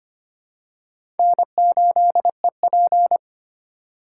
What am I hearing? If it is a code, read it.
Morse code, heard as a keyed tone, about 25 wpm, 700 Hz. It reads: N8EP
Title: USA Call Signs - 4 Characters - 3x Character Spacing - 25wpm